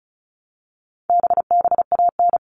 Morse code, heard as a keyed tone, about 34 wpm, 700 Hz. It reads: BBAN
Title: All Cut Numbers - 4-Digit 35wpm